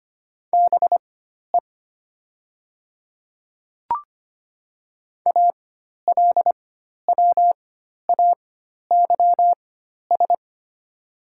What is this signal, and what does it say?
Morse code, heard as a keyed tone, about 25 wpm, spaced as for 12 wpm, 700 Hz. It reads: BE ALWAYS
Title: 4x Character Spacing - Top 300 Words - 25wpm